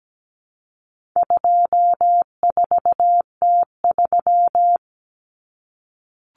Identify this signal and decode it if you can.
Morse code, heard as a keyed tone, about 17 wpm, 700 Hz. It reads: 24T3
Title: Common Cut Numbers - 4-Digit 17wpm